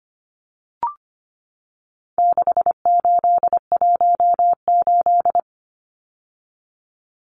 Morse code, heard as a keyed tone, about 25 wpm, 700 Hz. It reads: 6818
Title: Numbers - 4-Digit 25wpm